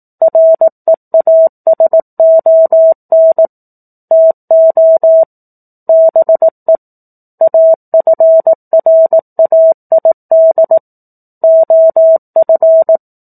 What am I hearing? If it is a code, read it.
Morse code, heard as a keyed tone, about 18 wpm, 650 Hz. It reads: REASON TO BE AFRAID OF
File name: 04-Chapter-04 18 WPM